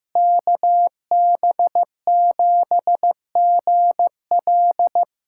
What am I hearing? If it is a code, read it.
Morse code, heard as a keyed tone, about 15 wpm, 700 Hz. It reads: KB7GL
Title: USA Callsign DE Callsign - 5 Characters - 15wpm